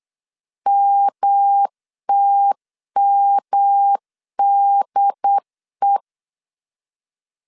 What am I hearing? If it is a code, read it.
Morse code, heard as a keyed tone, about 8 wpm, 800 Hz. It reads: MTMDE